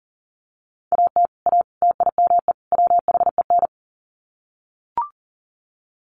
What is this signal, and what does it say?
Morse code, heard as a keyed tone, about 40 wpm, 700 Hz. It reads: AT A TIME WHEN
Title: Sets of 4 Words 40wpm